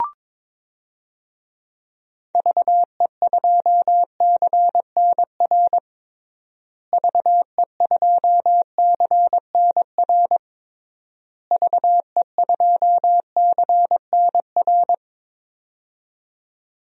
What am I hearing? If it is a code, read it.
Morse code, heard as a keyed tone, about 22 wpm, 700 Hz. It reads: VE2CNR VE2CNR VE2CNR
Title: DXCC-100 Call Signs - 6 Characters - Repeated 3x - 22wpm